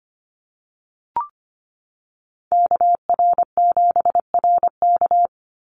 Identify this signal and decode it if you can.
Morse code, heard as a keyed tone, about 25 wpm, 690 Hz. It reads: KR7RK
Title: NA SOTA Activator Callsigns 25wpm